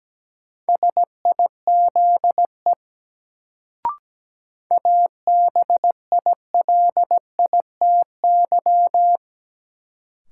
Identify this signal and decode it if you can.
Morse code, heard as a keyed tone, about 17 wpm, 700 Hz. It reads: SIZE ABILITY
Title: Sets of 3 Words - New Word Drill - 17wpm